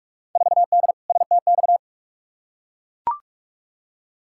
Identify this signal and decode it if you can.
Morse code, heard as a keyed tone, about 44 wpm, 700 Hz. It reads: 4D STX
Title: Field Day Exchanges Encore - 45wpm